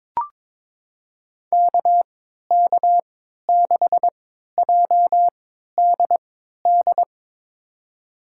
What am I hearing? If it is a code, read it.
Morse code, heard as a keyed tone, about 21 wpm, spaced as for 12 wpm, 700 Hz. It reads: KK6JDD
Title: USA Call Signs - 6 Characters - 3x Character Spacing - 22wpm